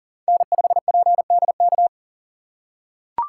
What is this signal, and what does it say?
Morse code, heard as a keyed tone, about 37 wpm, 700 Hz. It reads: N5PDK
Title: USA Call Signs - 5 Characters - 6x Character Spacing - 40wpm